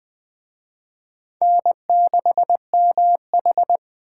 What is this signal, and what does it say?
Morse code, heard as a keyed tone, about 19 wpm, 700 Hz. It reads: N6MH